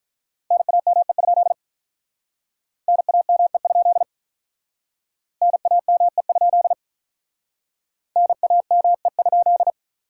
Morse code, heard as a keyed tone, about 46 wpm, 700 Hz. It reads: NAME? NAME? NAME? NAME?